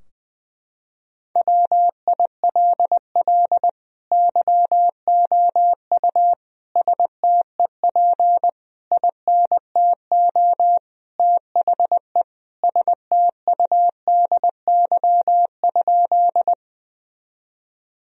Morse code, heard as a keyed tone, about 20 wpm, 700 Hz. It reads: WILL YOU STEP INTO THE STUDY?
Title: Sentences from Top 1000 Words - Review Part 2 - 20wpm